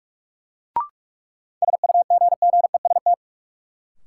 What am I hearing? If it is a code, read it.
Morse code, heard as a keyed tone, about 43 wpm, 700 Hz. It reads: SUGGEST
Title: Sentences from Top 1300 Words - New Word Drill - 45wpm